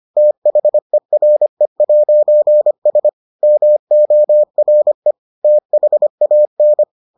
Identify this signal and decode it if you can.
Morse code, heard as a keyed tone, about 25 wpm, 600 Hz. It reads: THERE'S MORE THAN